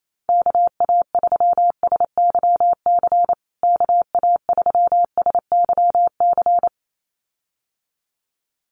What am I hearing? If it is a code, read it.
Morse code, heard as a keyed tone, about 28 wpm, 700 Hz. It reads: KA3SYC KA3SYC
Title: USA Call Signs Repeated 2x 28wpm